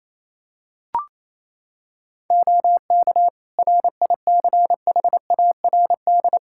Morse code, heard as a keyed tone, about 28 wpm, 700 Hz. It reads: OK RICHARD
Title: Practice QSO - Exchange 3 - More Info - 28wpm